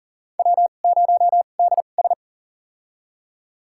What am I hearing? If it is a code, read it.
Morse code, heard as a keyed tone, about 37 wpm, spaced as for 28 wpm, 690 Hz. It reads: W0DS
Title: USA Call Signs - 4 Characters - 2x Character Spacing - 40wpm